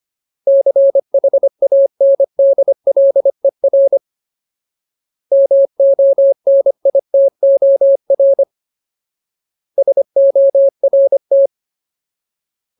Morse code, heard as a keyed tone, about 25 wpm, 550 Hz. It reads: CHANDLER MONITOR SORT
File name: Ultimate Word List / 25WPM Ultimate